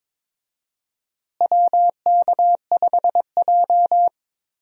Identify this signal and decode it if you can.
Morse code, heard as a keyed tone, about 22 wpm, 700 Hz. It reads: WK5J
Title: USA Call Signs - 4 Characters - Repeated 2x - 22wpm